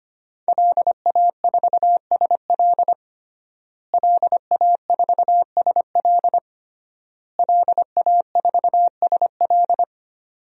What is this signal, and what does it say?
Morse code, heard as a keyed tone, about 25 wpm, 700 Hz. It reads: LA4SL LA4SL LA4SL
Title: DXCC-100 Call Signs - 5 Characters - Repeated 3x - 25wpm